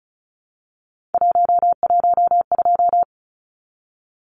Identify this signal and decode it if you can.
Morse code, heard as a keyed tone, about 34 wpm, 700 Hz. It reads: 112
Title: Numbers - 3-Digit 35wpm